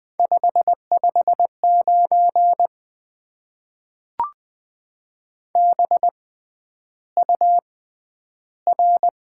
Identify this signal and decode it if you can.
Morse code, heard as a keyed tone, about 20 wpm, spaced as for 10 wpm, 700 Hz. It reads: 559 BUR
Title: Top 90 QSO Elements - 6x Character Spacing - 20wpm